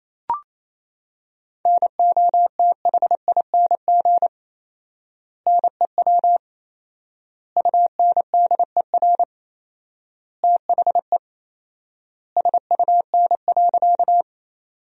Morse code, heard as a keyed tone, about 28 wpm, 700 Hz. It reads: NOTHING NEW UNDER THE SUN.
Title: 4x Word-Spacing - Sentences from Top 500 Words - 28wpm